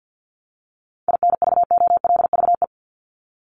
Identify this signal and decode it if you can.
Morse code, heard as a keyed tone, about 50 wpm, 700 Hz. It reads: INVOLVE
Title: ING Suffix - Top 100 Words with Base Words 50wpm